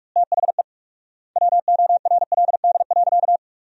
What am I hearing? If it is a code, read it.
Morse code, heard as a keyed tone, about 45 wpm, 700 Hz. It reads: THE WORLD.
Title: 4x Word-Spacing - Sentences from Top 200 Words - 45wpm